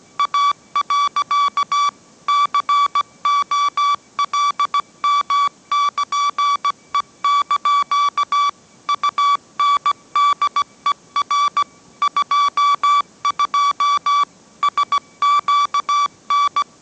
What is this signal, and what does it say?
Morse code, heard as a keyed tone, about 17 wpm, 1150 Hz. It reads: A. COLM<KN>E) UNDER 22 SQN
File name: mrx / mrx080 18wpm